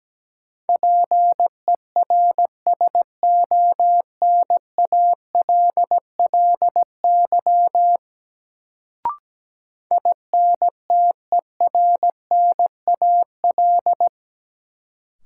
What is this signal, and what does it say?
Morse code, heard as a keyed tone, about 17 wpm, 700 Hz. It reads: PERSONALLY INTERNAL
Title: Sentences from Top 1400 Words - New Word Drill Alt - 17wpm